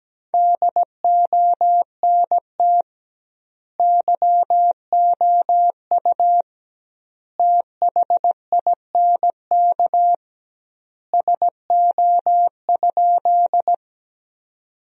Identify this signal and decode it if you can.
Morse code, heard as a keyed tone, about 17 wpm, 700 Hz. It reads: DONT YOU THINK SO?
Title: 2x Word-Spacing - Sentences from Top 300 Words - 17wpm